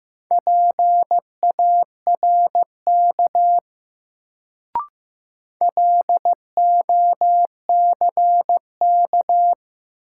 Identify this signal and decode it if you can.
Morse code, heard as a keyed tone, about 15 wpm, 700 Hz. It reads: PARK LOCK
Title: Sentences from Top 1300 Words - New Word Drill - 15wpm